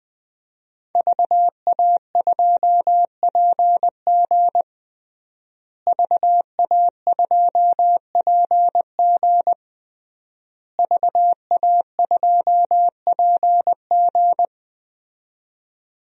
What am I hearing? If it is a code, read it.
Morse code, heard as a keyed tone, about 21 wpm, 700 Hz. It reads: VA2PG VA2PG VA2PG
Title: DXCC-100 Call Signs - 5 Characters - Repeated 3x - 20wpm